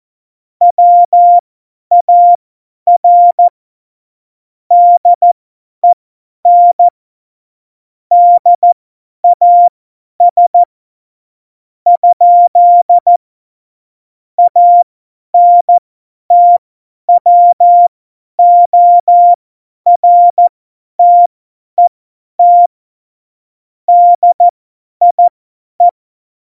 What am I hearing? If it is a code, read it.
Morse code, heard as a keyed tone, about 14 wpm, spaced as for 10 wpm, 700 Hz. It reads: WAR DEN DAS ? ANTWORTET DIE
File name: Morsekurs / Witze / Witz 29. Mai 2018, 50 F70